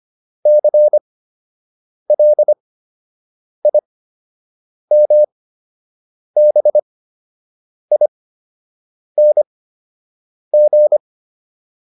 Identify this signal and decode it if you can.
Morse code, heard as a keyed tone, about 25 wpm, spaced as for 7 wpm, 600 Hz. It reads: CLIMBING